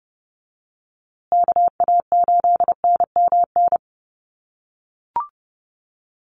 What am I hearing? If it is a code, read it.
Morse code, heard as a keyed tone, about 30 wpm, 700 Hz. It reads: KA8NMN
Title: USA Call Signs Repeated 3x 30wpm